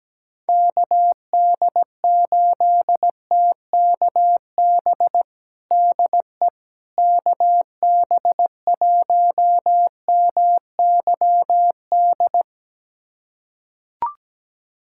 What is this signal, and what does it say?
Morse code, heard as a keyed tone, about 17 wpm, 700 Hz. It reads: KD8TKB DE KB1MYD
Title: USA Callsign DE Callsign - 6 Characters - 17wpm